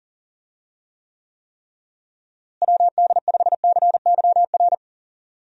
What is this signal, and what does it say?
Morse code, heard as a keyed tone, about 40 wpm, 700 Hz. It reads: WD5CYR
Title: USA Call Signs - 6 Characters - Repeated 3x - 40wpm